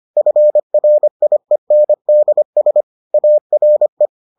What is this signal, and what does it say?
Morse code, heard as a keyed tone, about 25 wpm, 600 Hz. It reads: FRIENDS ARE